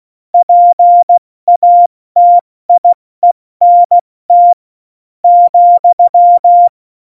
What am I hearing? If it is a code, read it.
Morse code, heard as a keyed tone, about 16 wpm, 700 Hz. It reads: PATIENT ,
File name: Morsekurs / Witze / Witz 20. Februar 2018, 70 F80